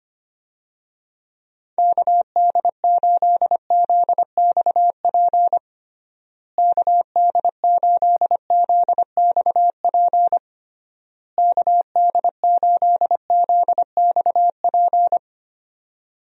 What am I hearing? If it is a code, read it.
Morse code, heard as a keyed tone, about 25 wpm, 700 Hz. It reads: KD8ZXP KD8ZXP KD8ZXP
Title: USA Call Signs - 6 Characters - Repeated 3x - 25wpm